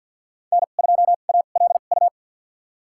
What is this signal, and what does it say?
Morse code, heard as a keyed tone, about 53 wpm, spaced as for 37 wpm, 700 Hz. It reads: N2ALU
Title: USA Call Signs - 5 Characters - 2x Character Spacing - 50wpm